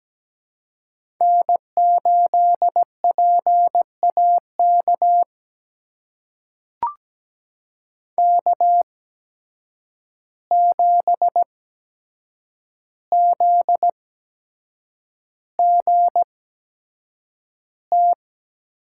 Morse code, heard as a keyed tone, about 17 wpm, spaced as for 7 wpm, 700 Hz. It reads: N8PAK K7ZGT
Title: USA Call Signs - 5 Characters - 8x Character Spacing - 17wpm